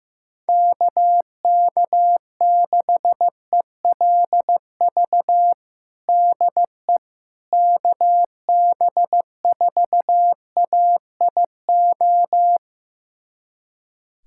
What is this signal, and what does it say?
Morse code, heard as a keyed tone, about 15 wpm, 700 Hz. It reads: KK6ELV DE KB4AIO
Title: USA Callsign DE Callsign - 6 Characters - 15wpm